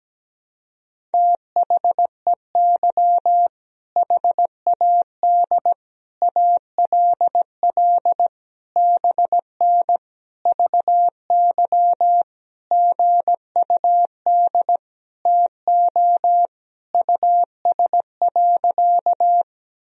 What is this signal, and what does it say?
Morse code, heard as a keyed tone, about 17 wpm, 700 Hz. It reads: THEY HAD ALL BN VY GUD TO US.
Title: Sentences from Top 200 Words and Abbreviations 17wpm